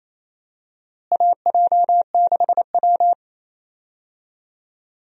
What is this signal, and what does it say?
Morse code, heard as a keyed tone, about 28 wpm, 700 Hz. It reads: AJ6W
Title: USA Call Signs - 4 Characters - Repeated 2x - 28wpm